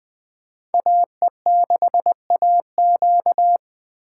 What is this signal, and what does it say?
Morse code, heard as a keyed tone, about 20 wpm, 700 Hz. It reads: AE6AQ